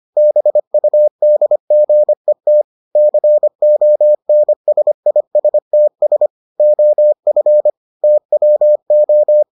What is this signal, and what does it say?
Morse code, heard as a keyed tone, about 25 wpm, 600 Hz. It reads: BUDGET CONSISTS OF TWO